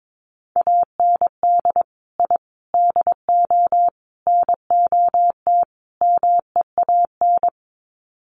Morse code, heard as a keyed tone, about 22 wpm, 700 Hz. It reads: AND I DO NOT MEAN
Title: Sets of 5 Words Encore2 22wpm